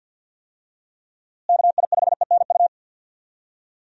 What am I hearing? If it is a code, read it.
Morse code, heard as a keyed tone, about 48 wpm, 700 Hz. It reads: KI5ENU
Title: USA Call Signs - 6 Characters - Repeated 2x - 50wpm